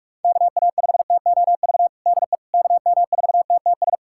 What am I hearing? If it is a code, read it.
Morse code, heard as a keyed tone, about 43 wpm, 690 Hz. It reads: KA5TOV DE KM4TTS
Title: USA Callsign DE Callsign - 6 Characters - 45wpm